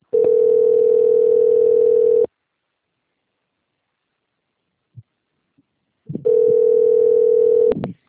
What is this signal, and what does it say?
Ringback tone (North America), 440 + 480 Hz.